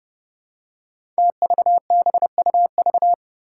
Morse code, heard as a keyed tone, about 30 wpm, 700 Hz. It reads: TVBUV
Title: Five Challenging Letter-Number Combinations 30wpm